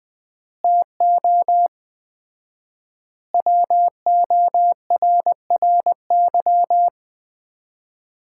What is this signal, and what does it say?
Morse code, heard as a keyed tone, about 20 wpm, 700 Hz. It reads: TO WORRY
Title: 4x Word-Spacing - Sets of 5 Words - 20wpm